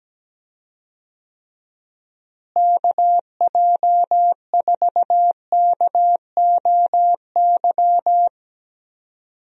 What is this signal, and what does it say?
Morse code, heard as a keyed tone, about 17 wpm, 700 Hz. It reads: KJ4KOY
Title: USA Call Signs - 6 Characters - Repeated 3x - 17wpm